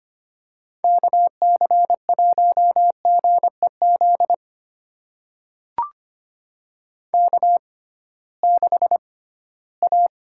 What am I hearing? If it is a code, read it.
Morse code, heard as a keyed tone, about 25 wpm, spaced as for 16 wpm, 700 Hz. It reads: KC1GEZ K6A
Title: USA Call Signs - 6x Character Spacing - 25wpm